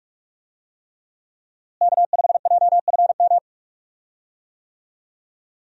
Morse code, heard as a keyed tone, about 46 wpm, 700 Hz. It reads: K5JFM
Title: USA Call Signs - 5 Characters - Repeated 3x - 45wpm